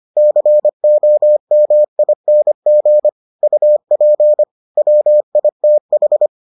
Morse code, heard as a keyed tone, about 25 wpm, 600 Hz. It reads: COMING UP WITH